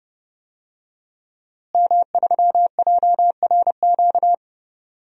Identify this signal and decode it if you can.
Morse code, heard as a keyed tone, about 30 wpm, 700 Hz. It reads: M3JRQ
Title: DXCC-100 Call Signs - 5 Characters - Repeated 2x - 30wpm